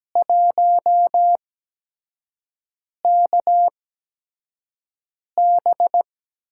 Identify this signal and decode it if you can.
Morse code, heard as a keyed tone, about 17 wpm, spaced as for 5 wpm, 700 Hz. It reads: 1KB
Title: USA Call Signs - 8x Character Spacing - 17wpm